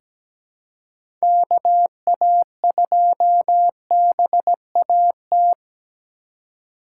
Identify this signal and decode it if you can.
Morse code, heard as a keyed tone, about 17 wpm, 700 Hz. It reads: KA2BAT
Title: USA Call Signs - 6 Characters - 4x Character Spacing - 17wpm